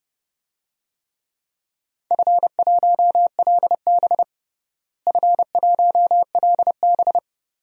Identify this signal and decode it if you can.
Morse code, heard as a keyed tone, about 30 wpm, 700 Hz. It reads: F1LB F1LB
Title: DXCC-100 Call Signs - 4 Characters - Repeated 2x - 30wpm